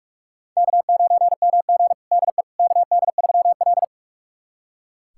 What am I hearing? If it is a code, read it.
Morse code, heard as a keyed tone, about 41 wpm, 700 Hz. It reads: K9MG DE KD3L
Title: USA Callsign DE Callsign - 4 Characters - 45wpm